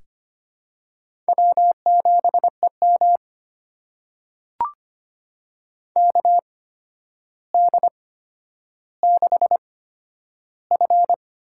Morse code, heard as a keyed tone, about 25 wpm, spaced as for 11 wpm, 700 Hz. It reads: W7EM KD6F
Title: USA Call Signs - 4 Characters - 8x Character Spacing - 25wpm